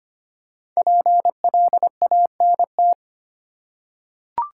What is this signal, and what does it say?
Morse code, heard as a keyed tone, about 25 wpm, 700 Hz. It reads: PLANT
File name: 5 Letter Words - First 3 Letters Given - 25wpm